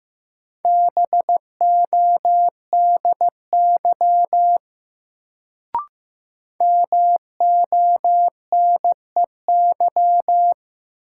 Morse code, heard as a keyed tone, about 15 wpm, 700 Hz. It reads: BODY MONEY
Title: Sentences from Top 400 Words - New Word Drill - 15wpm